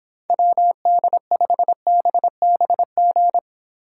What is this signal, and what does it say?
Morse code, heard as a keyed tone, about 26 wpm, 700 Hz. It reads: WD5BBG